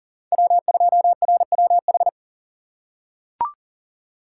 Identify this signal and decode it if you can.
Morse code, heard as a keyed tone, about 37 wpm, 700 Hz. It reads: W2RWH
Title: USA Call Signs - 6x Character Spacing - 40wpm